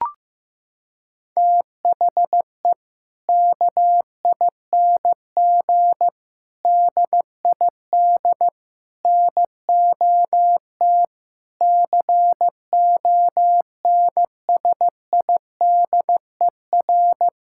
Morse code, heard as a keyed tone, about 16 wpm, 690 Hz. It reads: THE KING DID NOT CONSIDER
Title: Sentences from Top 1000 Words - Review Part 5 - 15wpm